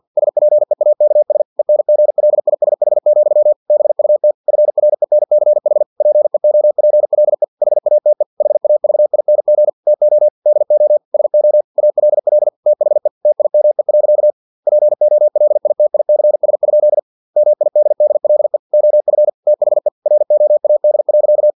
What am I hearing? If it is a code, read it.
Morse code, heard as a keyed tone, about 50 wpm, 600 Hz. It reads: SPEAKS ENGLISH, BUT FRENCH PEOPLE HATE HAVING TO DO SO ALL THE TIME. POLITICS? MIDDLE OF THE ROAD.
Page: Never 49wpm